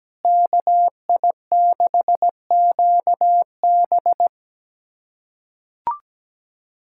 Morse code, heard as a keyed tone, about 17 wpm, 700 Hz. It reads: KI6QB
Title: USA Callsign DE Callsign - 5 Characters - 17wpm